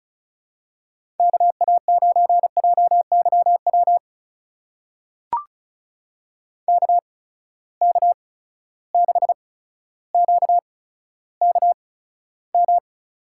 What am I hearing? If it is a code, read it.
Morse code, heard as a keyed tone, about 36 wpm, spaced as for 15 wpm, 700 Hz. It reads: KA9JYW KK6QKM
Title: USA Call Signs - 8x Character Spacing - 35wpm